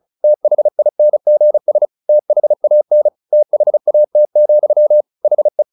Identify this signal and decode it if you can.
Morse code, heard as a keyed tone, about 34 wpm, 600 Hz. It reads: THINGS THAN THAT, HE
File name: the old man and the sea 2 35wpm